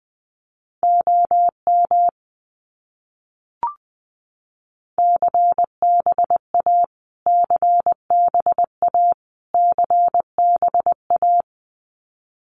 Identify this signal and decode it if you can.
Morse code, heard as a keyed tone, about 19 wpm, 700 Hz. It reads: OM CBA CBA CBA
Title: Abbreviations Repeated 3x - 20wpm